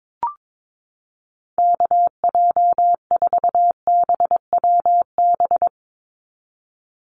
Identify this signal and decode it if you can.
Morse code, heard as a keyed tone, about 22 wpm, 700 Hz. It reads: KJ4BWB
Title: USA Call Signs 22wpm